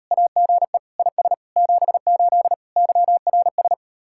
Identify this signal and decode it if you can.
Morse code, heard as a keyed tone, about 38 wpm, 700 Hz. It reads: AGE IS 78 YRS